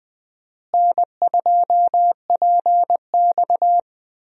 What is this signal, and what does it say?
Morse code, heard as a keyed tone, about 20 wpm, 700 Hz. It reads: N2PX
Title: USA Call Signs - 4 Characters - Repeated 3x - 20wpm